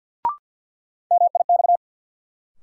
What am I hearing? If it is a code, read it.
Morse code, heard as a keyed tone, about 52 wpm, 700 Hz. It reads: MIX
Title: Sentences from Top 1100 Words - New Word Drill Alt - 50wpm